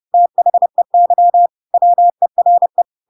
Morse code, heard as a keyed tone, about 30 wpm, 700 Hz. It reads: THEY WERE
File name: Pinocchio / Pinocchio ch3 30 30 ik1hge